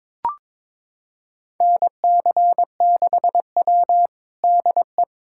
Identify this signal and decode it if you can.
Morse code, heard as a keyed tone, about 22 wpm, 700 Hz. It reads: NC6W DE
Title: USA Callsign DE Callsign - 4 Characters - 22wpm